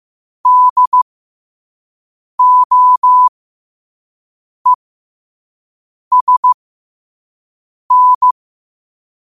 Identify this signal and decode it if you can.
Morse code, heard as a keyed tone, about 15 wpm, spaced as for 5 wpm, 1000 Hz. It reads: DOESN
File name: qotdfiles / 5wpm